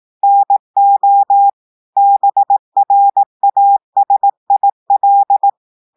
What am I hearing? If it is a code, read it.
Morse code, heard as a keyed tone, about 18 wpm, 800 Hz. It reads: NO BRASIL
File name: audios / portugais / portugais texte en clair 1 vitesse 18 mots minutes